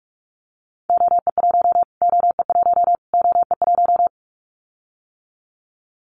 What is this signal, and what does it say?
Morse code, heard as a keyed tone, about 44 wpm, 700 Hz. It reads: OE1 OE1 OE1
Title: Callsign Prefixes for Top 25 DXCC - Repeated 3x - 45wpm